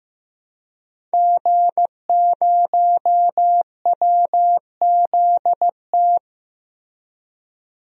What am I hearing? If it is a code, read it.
Morse code, heard as a keyed tone, about 15 wpm, 700 Hz. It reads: G0WZT
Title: DXCC-100 Call Signs - 5 Characters - Repeated 2x - 15wpm